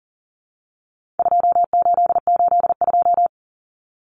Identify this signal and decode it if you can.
Morse code, heard as a keyed tone, about 40 wpm, 700 Hz. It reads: 2882
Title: Common Cut Numbers - 4-Digit 40wpm